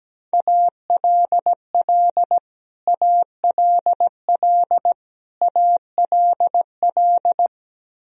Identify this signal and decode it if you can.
Morse code, heard as a keyed tone, about 17 wpm, 700 Hz. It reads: ALL ALL ALL
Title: Top 100 Words Repeated 17wpm